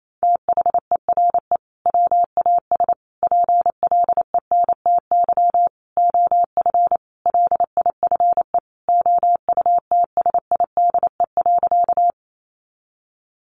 Morse code, heard as a keyed tone, about 28 wpm, 700 Hz. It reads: THERE WAS PLENTY OF LIFE OUTSIDE.
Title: Sentences from Top 1300 Words 28wpm